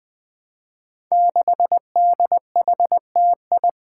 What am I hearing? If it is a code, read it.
Morse code, heard as a keyed tone, about 20 wpm, 700 Hz. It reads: 6DHTI
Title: Five Challenging Letter-Number Combinations 20wpm